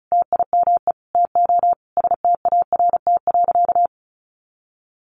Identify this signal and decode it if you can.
Morse code, heard as a keyed tone, about 34 wpm, 700 Hz. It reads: TIME TO START.
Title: Sentences from Top 300 Words 35wpm